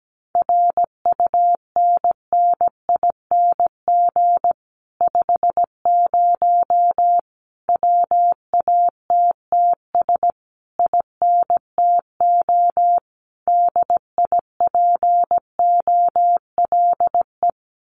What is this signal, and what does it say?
Morse code, heard as a keyed tone, about 17 wpm, 700 Hz. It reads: RUNNING 50 WATTS INTO DIPOLE
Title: Common QSO Phrases Encore 17wpm